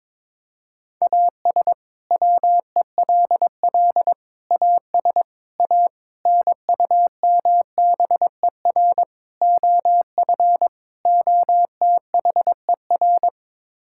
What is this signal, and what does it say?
Morse code, heard as a keyed tone, about 22 wpm, 700 Hz. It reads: AS WELL AS A NUMBER OF OTHER
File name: practice / 22wpm / Sets of 7 Words - 22wpm